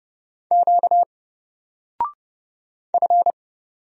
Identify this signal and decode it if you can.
Morse code, heard as a keyed tone, about 30 wpm, 700 Hz. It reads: Q F
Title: Single Letter-Number Alt 30wpm